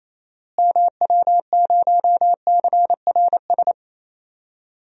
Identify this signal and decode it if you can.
Morse code, heard as a keyed tone, about 28 wpm, 700 Hz. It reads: MW0CRS
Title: DXCC-100 Call Signs - 6 Characters - Repeated 2x - 28wpm